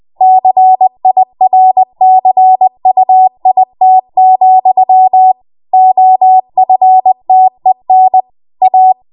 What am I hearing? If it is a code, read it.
Morse code, heard as a keyed tone, about 20 wpm, 750 Hz. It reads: CIRCUIT, OFTEN A
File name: Morse / 20WPM